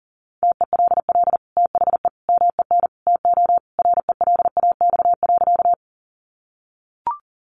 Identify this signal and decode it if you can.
Morse code, heard as a keyed tone, about 40 wpm, 700 Hz. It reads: TELL THE MEN TO RELAX.
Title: Sentences from Top 1400 Words 40wpm